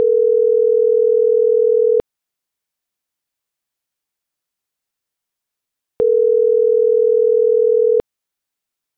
Ringback tone (North America), 440 + 480 Hz.